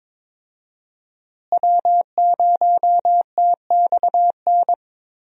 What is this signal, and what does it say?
Morse code, heard as a keyed tone, about 21 wpm, 700 Hz. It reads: W0TXN
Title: USA Call Signs - 5 Characters - Repeated 3x - 22wpm